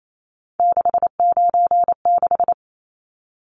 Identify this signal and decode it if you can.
Morse code, heard as a keyed tone, about 28 wpm, 700 Hz. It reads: 696
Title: Numbers - 3-Digit 28wpm